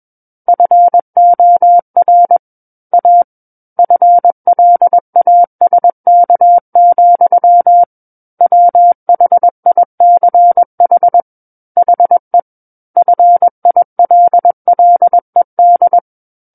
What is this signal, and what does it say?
Morse code, heard as a keyed tone, about 21 wpm, 700 Hz. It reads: FOR A FLASK, WHICH HE FILLED
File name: Book One / 12-Chapter 12 - 21 WPM